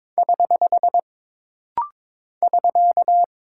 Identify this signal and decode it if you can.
Morse code, heard as a keyed tone, about 22 wpm, 700 Hz. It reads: <HH> <SK>
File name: practice / 22wpm / Single Punctuation-Prosign - Rapid-Fire - 22wpm